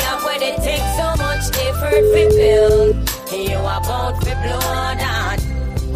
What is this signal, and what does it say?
Ringback tone (Europe, 425 Hz), with noise.